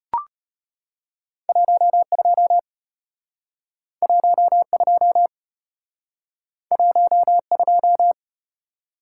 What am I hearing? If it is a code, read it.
Morse code, heard as a keyed tone, about 33 wpm, 700 Hz. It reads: 12 12 12